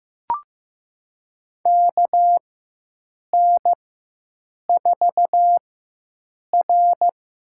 Morse code, heard as a keyed tone, about 15 wpm, spaced as for 7 wpm, 700 Hz. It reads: KN4R
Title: USA Call Signs - 4x Character Spacing - 15wpm